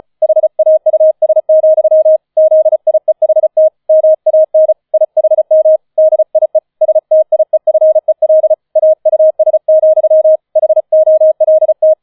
Morse code, heard as a keyed tone, about 34 wpm, 630 Hz. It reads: HAUS, ZIEHT MAN IHM DIE STIEFEL AUS, HOLT
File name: 005 Fünfter Streich 35wpm